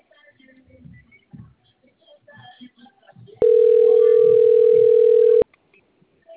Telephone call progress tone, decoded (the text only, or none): ringback tone